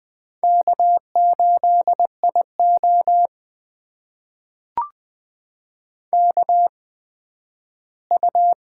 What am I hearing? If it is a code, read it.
Morse code, heard as a keyed tone, about 20 wpm, spaced as for 12 wpm, 700 Hz. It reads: K8IO KU
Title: USA Call Signs - 4 Characters - 8x Character Spacing - 20wpm